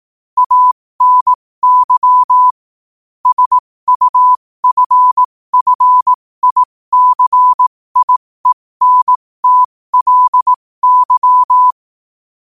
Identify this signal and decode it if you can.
Morse code, heard as a keyed tone, about 18 wpm, spaced as for 15 wpm, 1000 Hz. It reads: ANY SUFFICIENTLY